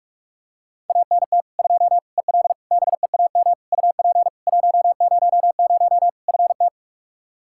Morse code, heard as a keyed tone, about 46 wpm, 700 Hz. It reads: ANT 2 EL BEAM UP 100 FT
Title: K1USN SST QSO Practice 45wpm